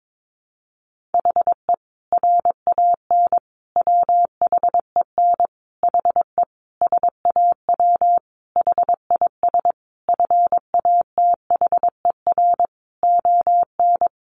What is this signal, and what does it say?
Morse code, heard as a keyed tone, about 22 wpm, 700 Hz. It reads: HE RAN WHEN HE SAW HIS FATHER ON
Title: Sentences from Top 500 Words 22wpm